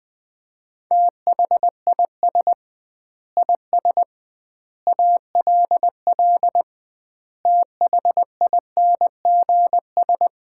Morse code, heard as a keyed tone, about 20 wpm, 700 Hz. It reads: THIS IS ALL THINGS
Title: 2x Word-Spacing - Sets of 5 Words - 20wpm